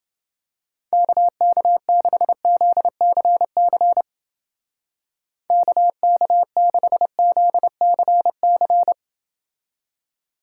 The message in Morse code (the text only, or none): KK6ZCC KK6ZCC